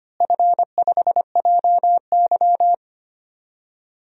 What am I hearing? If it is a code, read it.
Morse code, heard as a keyed tone, about 25 wpm, 700 Hz. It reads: F5JY
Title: DXCC-100 Call Signs - 4 Characters - Repeated 2x - 25wpm